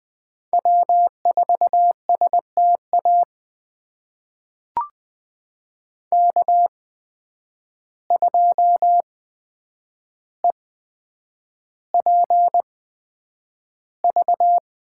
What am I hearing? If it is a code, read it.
Morse code, heard as a keyed tone, about 20 wpm, spaced as for 9 wpm, 700 Hz. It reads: W4STA K2EPV
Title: USA Call Signs - 5 Characters - 8x Character Spacing - 20wpm